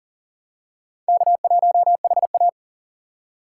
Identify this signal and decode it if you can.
Morse code, heard as a keyed tone, about 39 wpm, 700 Hz. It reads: K1HA
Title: USA Call Signs - 4 Characters - 3x Character Spacing - 40wpm